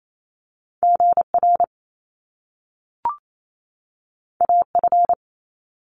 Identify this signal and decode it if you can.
Morse code, heard as a keyed tone, about 28 wpm, 700 Hz. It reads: GR AF
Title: Top Two Letter Combinations in Words 28wpm